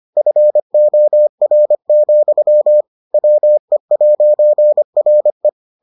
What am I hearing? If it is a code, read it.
Morse code, heard as a keyed tone, about 25 wpm, 600 Hz. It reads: FOR, WE'RE